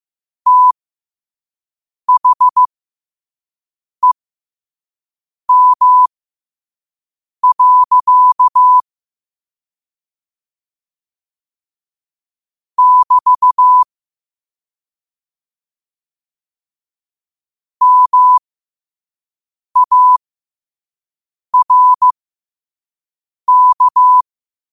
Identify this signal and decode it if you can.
Morse code, heard as a keyed tone, about 15 wpm, spaced as for 5 wpm, 1000 Hz. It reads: THEM. <BT> MARK
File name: qotdfiles / 5wpm